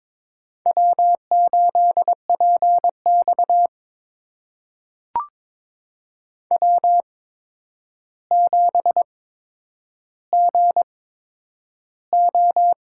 Morse code, heard as a keyed tone, about 22 wpm, spaced as for 9 wpm, 700 Hz. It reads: W8PX W7GO
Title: USA Call Signs - 4 Characters - 8x Character Spacing - 22wpm